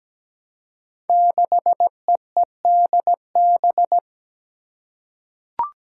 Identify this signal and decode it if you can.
Morse code, heard as a keyed tone, about 17 wpm, 700 Hz. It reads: 6EEDB
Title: Five Challenging Letter-Number Combinations 17wpm